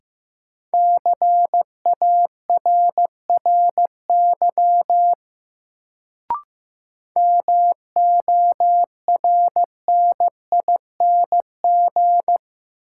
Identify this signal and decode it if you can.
Morse code, heard as a keyed tone, about 15 wpm, 700 Hz. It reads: CARRY MORNING
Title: Sentences from Top 400 Words - New Word Drill - 15wpm